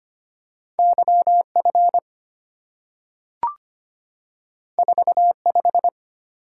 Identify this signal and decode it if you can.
Morse code, heard as a keyed tone, about 25 wpm, 700 Hz. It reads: YF 45
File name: Two Letter-Number Combinations - 25wpm